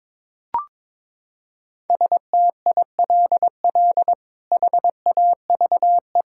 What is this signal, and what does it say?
Morse code, heard as a keyed tone, about 22 wpm, 700 Hz. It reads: STILL HAVE
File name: practice / 22wpm / Sets of 7 Words - 22wpm